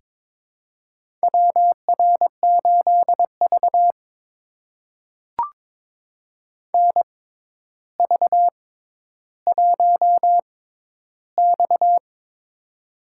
Morse code, heard as a keyed tone, about 22 wpm, spaced as for 11 wpm, 700 Hz. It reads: WR8V NV1X
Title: USA Call Signs - 4 Characters - 6x Character Spacing - 22wpm